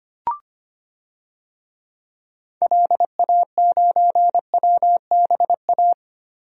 Morse code, heard as a keyed tone, about 25 wpm, 700 Hz. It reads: LA9WBA
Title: DXCC-100 Call Signs - 6 Characters - Repeated 3x - 25wpm